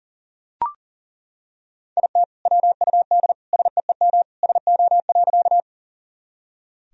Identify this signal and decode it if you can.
Morse code, heard as a keyed tone, about 40 wpm, 700 Hz. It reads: IT WUD SEEM SO.
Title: Sentences from Top 300 Words and Abbreviations 40wpm